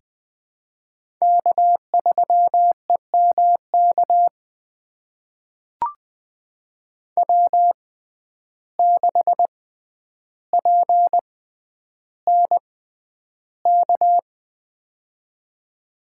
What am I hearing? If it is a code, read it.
Morse code, heard as a keyed tone, about 20 wpm, spaced as for 10 wpm, 700 Hz. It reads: K3EMK W6PNK
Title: USA Call Signs - 5 Characters - 6x Character Spacing - 20wpm